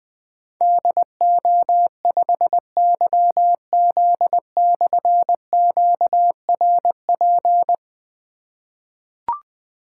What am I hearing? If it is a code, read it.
Morse code, heard as a keyed tone, about 20 wpm, 700 Hz. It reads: DO5YZ/QRP
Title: Difficult Call Signs 20wpm